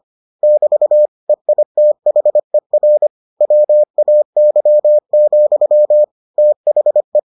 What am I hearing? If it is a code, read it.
Morse code, heard as a keyed tone, about 25 wpm, 600 Hz. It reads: <BT> EITHER WAY, THE